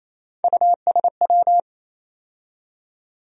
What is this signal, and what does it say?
Morse code, heard as a keyed tone, about 28 wpm, 700 Hz. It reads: USW
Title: USA Call Sign Postfixes - 3 Letters 28wpm